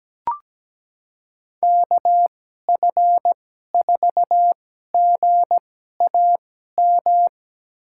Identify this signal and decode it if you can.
Morse code, heard as a keyed tone, about 17 wpm, spaced as for 12 wpm, 700 Hz. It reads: KF4GAM
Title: USA Call Signs - 2x Character Spacing - 17wpm